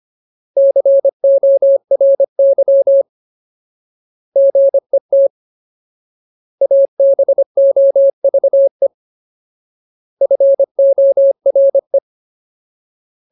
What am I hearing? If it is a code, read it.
Morse code, heard as a keyed tone, about 25 wpm, 550 Hz. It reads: CORY GET ABOVE FORE
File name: Ultimate Word List / 25WPM Ultimate